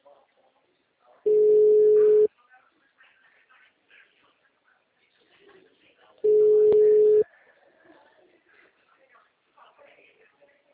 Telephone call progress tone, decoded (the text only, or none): ringback tone